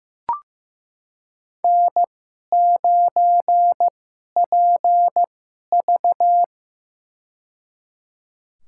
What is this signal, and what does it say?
Morse code, heard as a keyed tone, about 15 wpm, spaced as for 11 wpm, 700 Hz. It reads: N9PV
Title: USA Call Signs - 4 Characters - 2x Character Spacing - 15wpm